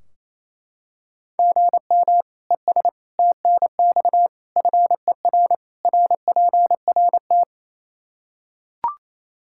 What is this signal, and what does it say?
Morse code, heard as a keyed tone, about 28 wpm, 700 Hz. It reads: GM ES TNX FER RPRT
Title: Practice QSO - Exchange 1 - Beginning - 28wpm